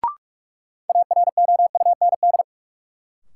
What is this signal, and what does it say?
Morse code, heard as a keyed tone, about 41 wpm, 700 Hz. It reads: AROUND
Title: Sentences from Top 600 Words - New Word Drill - 45wpm